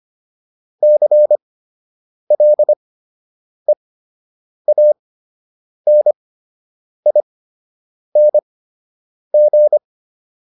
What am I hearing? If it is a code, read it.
Morse code, heard as a keyed tone, about 25 wpm, spaced as for 8 wpm, 600 Hz. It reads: CLEANING
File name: ING8 08